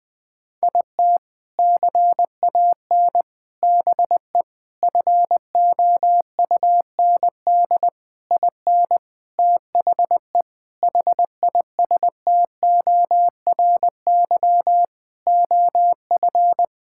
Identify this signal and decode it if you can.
Morse code, heard as a keyed tone, about 20 wpm, 700 Hz. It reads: IT CAN BE FOUND IN THE HISTORY OF
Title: Sentences with 4-Word Phrases - 20wpm